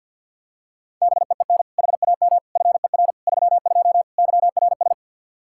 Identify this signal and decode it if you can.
Morse code, heard as a keyed tone, about 51 wpm, 700 Hz. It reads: BEEN HAM FER 32 YRS
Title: K1USN SST QSO Practice 50wpm